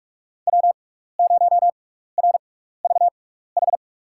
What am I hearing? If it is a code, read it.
Morse code, heard as a keyed tone, about 46 wpm, spaced as for 15 wpm, 700 Hz. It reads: W0RVH
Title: USA Call Signs - 5 Characters - 6x Character Spacing - 45wpm